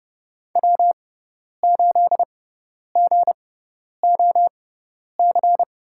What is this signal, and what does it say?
Morse code, heard as a keyed tone, about 31 wpm, spaced as for 10 wpm, 700 Hz. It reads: W8GOC
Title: USA Call Signs - 6x Character Spacing - 30wpm